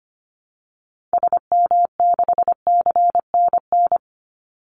Morse code, heard as a keyed tone, about 25 wpm, 700 Hz. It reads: SM6CNN
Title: Call Signs Repeated 2x - 25wpm